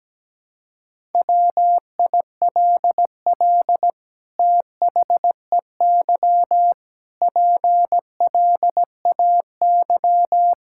Morse code, heard as a keyed tone, about 17 wpm, 700 Hz. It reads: WILL THEY PLAY
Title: Sentences from Top 1000 Words - Review Part 6 - 17wpm